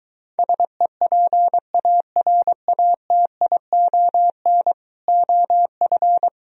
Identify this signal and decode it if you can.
Morse code, heard as a keyed tone, about 24 wpm, 700 Hz. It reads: SEPARATION OF